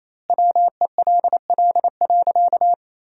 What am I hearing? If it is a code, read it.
Morse code, heard as a keyed tone, about 28 wpm, 700 Hz. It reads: WELL.
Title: 2x Word-Spacing - Sentences from Top 500 Words - 28wpm